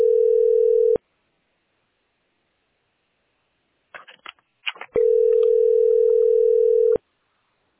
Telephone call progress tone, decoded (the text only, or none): ringback tone